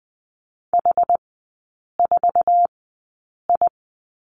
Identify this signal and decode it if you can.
Morse code, heard as a keyed tone, about 20 wpm, spaced as for 8 wpm, 700 Hz. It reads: H4I
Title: 074 Lesson 13e - Letter L - Three Characters 20wpm